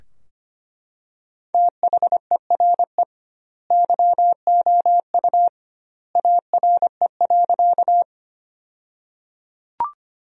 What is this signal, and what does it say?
Morse code, heard as a keyed tone, about 25 wpm, 700 Hz. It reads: THERE YOU ARE.
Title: 2x Word-Spacing - Sentences from Top 100 Words - 25wpm